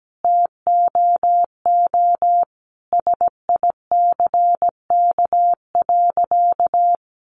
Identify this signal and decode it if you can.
Morse code, heard as a keyed tone, about 17 wpm, 700 Hz. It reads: TOO SICK.
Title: Easy Phrases 17wpm